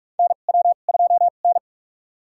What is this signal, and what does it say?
Morse code, heard as a keyed tone, about 43 wpm, spaced as for 32 wpm, 700 Hz. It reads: NW2N
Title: USA Call Signs - 2x Character Spacing - 45wpm